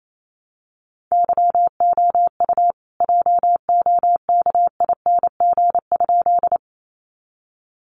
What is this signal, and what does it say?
Morse code, heard as a keyed tone, about 28 wpm, 700 Hz. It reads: YOU JOKING?
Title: A Sound Of Thunder 28wpm